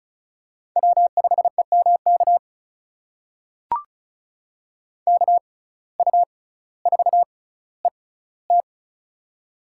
Morse code, heard as a keyed tone, about 36 wpm, spaced as for 18 wpm, 700 Hz. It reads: W5EMK KU4ET
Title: USA Call Signs - 5 Characters - 6x Character Spacing - 35wpm